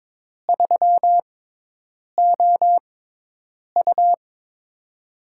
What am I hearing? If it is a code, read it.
Morse code, heard as a keyed tone, about 22 wpm, spaced as for 8 wpm, 700 Hz. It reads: 3OU
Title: USA Call Signs - 6x Character Spacing - 22wpm